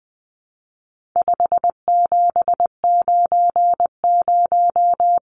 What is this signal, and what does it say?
Morse code, heard as a keyed tone, about 20 wpm, 700 Hz. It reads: 5790
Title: Numbers - 4-Digit 20wpm